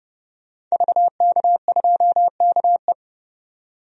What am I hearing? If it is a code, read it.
Morse code, heard as a keyed tone, about 30 wpm, 700 Hz. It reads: VK2KE
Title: DXCC-100 Call Signs - 5 Characters - 30wpm